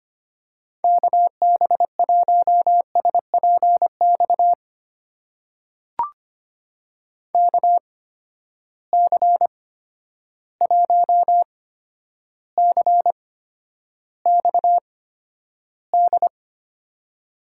Morse code, heard as a keyed tone, about 25 wpm, spaced as for 11 wpm, 700 Hz. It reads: KB1SPX KC1CXD
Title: USA Call Signs - 6 Characters - 8x Character Spacing - 25wpm